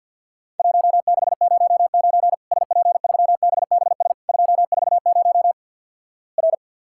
Morse code, heard as a keyed tone, about 50 wpm, 700 Hz. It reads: 1609 SP3BBS 240 R
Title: Worked All Europe DX Contest - QTC Exchange with Repeats - 50wpm